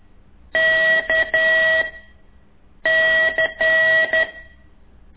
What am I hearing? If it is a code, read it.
Morse code, heard as a keyed tone, about 11 wpm, 1950 Hz. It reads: K C
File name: ik1weg2 / CORSI CW